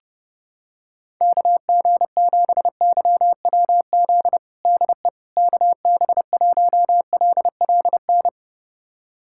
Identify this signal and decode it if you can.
Morse code, heard as a keyed tone, about 30 wpm, 700 Hz. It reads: KG7YWZ DE KB1LLN
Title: USA Callsign DE Callsign - 6 Characters - 30wpm